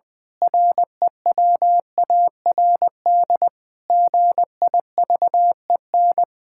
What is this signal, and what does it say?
Morse code, heard as a keyed tone, about 20 wpm, 700 Hz. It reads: REWARD GIVEN
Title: Pangram Challenge v2 - 20wpm